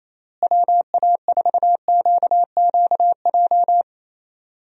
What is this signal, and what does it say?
Morse code, heard as a keyed tone, about 28 wpm, 700 Hz. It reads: WA4QQJ
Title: USA Call Signs - 6 Characters - Repeated 2x - 28wpm